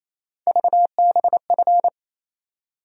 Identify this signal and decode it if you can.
Morse code, heard as a keyed tone, about 28 wpm, 700 Hz. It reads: VBF
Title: USA Call Sign Postfixes - 3 Letters 28wpm